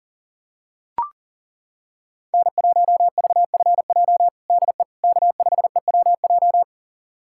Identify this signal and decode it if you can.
Morse code, heard as a keyed tone, about 40 wpm, 700 Hz. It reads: N1VFJ DE K5EWJ
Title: USA Callsign DE Callsign - 5 Characters - 40wpm